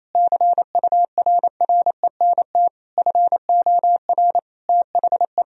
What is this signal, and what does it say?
Morse code, heard as a keyed tone, about 28 wpm, 700 Hz. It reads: CURRENT FOR THE
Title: Sentences from Top 1000 Words - Review Part 4 - 28wpm